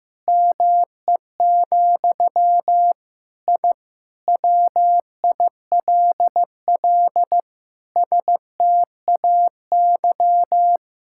Morse code, heard as a keyed tone, about 15 wpm, 700 Hz. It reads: ME, I WILL STAY
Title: Sentences from Top 1000 Words - Review Part 6 - 15wpm